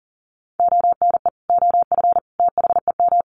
Morse code, heard as a keyed tone, about 37 wpm, 700 Hz. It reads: ONE OF THEM
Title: Sets of 3 Words 40wpm